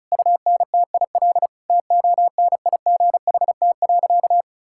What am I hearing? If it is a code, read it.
Morse code, heard as a keyed tone, about 36 wpm, 700 Hz. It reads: UNTIL TONIGHT.
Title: Sentences from Top 1000 Words - Review Part 5 - 35wpm